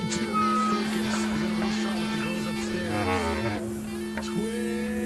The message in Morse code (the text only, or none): none